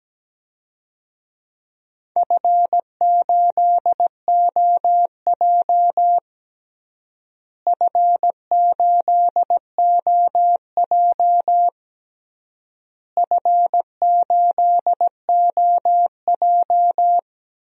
Morse code, heard as a keyed tone, about 17 wpm, 700 Hz. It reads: F8OJ F8OJ F8OJ
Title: DXCC-100 Call Signs - 4 Characters - Repeated 3x - 17wpm